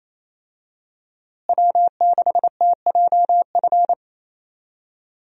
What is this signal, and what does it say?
Morse code, heard as a keyed tone, about 28 wpm, 700 Hz. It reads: W6TJF
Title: USA Call Signs - 5 Characters - Repeated 3x - 28wpm